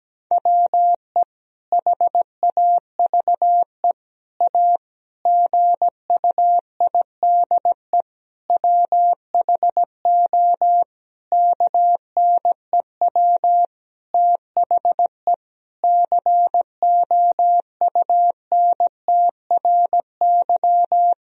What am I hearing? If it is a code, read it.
Morse code, heard as a keyed tone, about 17 wpm, 700 Hz. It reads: WE HAVE A GUIDE WHO KNEW THE COUNTRY
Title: Sentences from Top 1000 Words - Review Part 3 - 17wpm